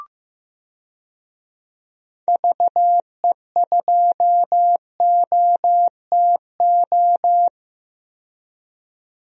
Morse code, heard as a keyed tone, about 16 wpm, 700 Hz. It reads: VE2OTO
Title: DXCC-100 Call Signs - 6 Characters - 15wpm